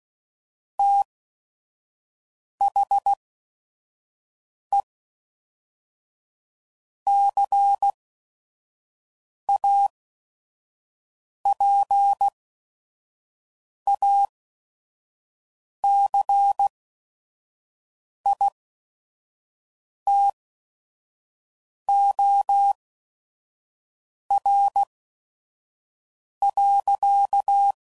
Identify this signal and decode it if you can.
Morse code, heard as a keyed tone, about 16 wpm, spaced as for 5 wpm, 800 Hz. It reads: THE CAPACITOR.
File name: CW Code Groups @5WPM / Technical Subjects / Reactance